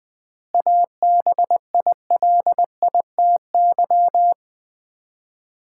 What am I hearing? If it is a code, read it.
Morse code, heard as a keyed tone, about 20 wpm, 700 Hz. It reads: ABILITY
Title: 4x Word-Spacing - Sets of 3 Words - 20wpm